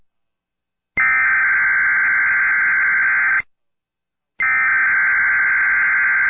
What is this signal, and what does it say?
Emergency Alert System (SAME) tones (header bursts), heard as harsh buzzing bursts. It is heard from 1.0 to 6.0 s.